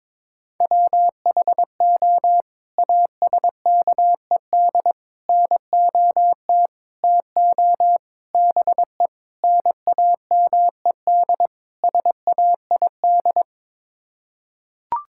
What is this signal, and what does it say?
Morse code, heard as a keyed tone, about 22 wpm, 700 Hz. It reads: WHO ASKED NOT TO BE NAMED SAID
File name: practice / 22wpm / Sets of 7 Words - 22wpm